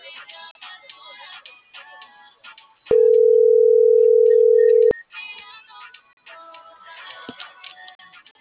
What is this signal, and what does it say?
Ringback tone (North America), 440 + 480 Hz.